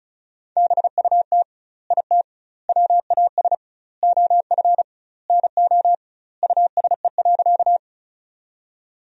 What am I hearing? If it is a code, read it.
Morse code, heard as a keyed tone, about 34 wpm, 700 Hz. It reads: BUT IT WAS OF NO USE.
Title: 2x Word-Spacing - Sentences from Top 100 Words - 35wpm